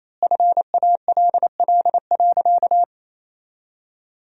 Morse code, heard as a keyed tone, about 28 wpm, 700 Hz. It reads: FALL.
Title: 4x Word-Spacing - Sentences from Top 500 Words - 28wpm